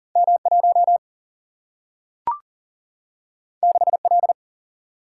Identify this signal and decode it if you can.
Morse code, heard as a keyed tone, about 40 wpm, 700 Hz. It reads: M 1 6 L
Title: Single-Letter Single-Number 40wpm